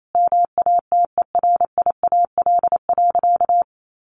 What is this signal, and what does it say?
Morse code, heard as a keyed tone, about 28 wpm, 700 Hz. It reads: MATERIAL.
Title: Sentences from Top 1100 Words 28wpm